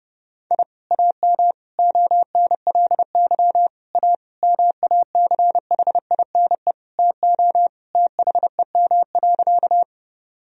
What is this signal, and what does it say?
Morse code, heard as a keyed tone, about 30 wpm, 700 Hz. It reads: I AM ONLY A MACHINE TO THEM.
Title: Sentences from Top 1000 Words - Review Part 2 - 30wpm